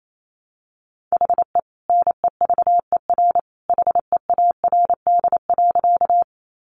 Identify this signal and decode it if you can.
Morse code, heard as a keyed tone, about 28 wpm, 700 Hz. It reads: HE NEVER HEARD.
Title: Sentences from Top 400 Words 28wpm